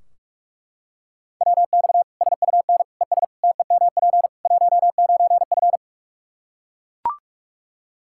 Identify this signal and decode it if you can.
Morse code, heard as a keyed tone, about 41 wpm, 700 Hz. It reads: WX SUN ES TEMP 19F
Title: Practice QSO - Exchange 2 - Info - 45wpm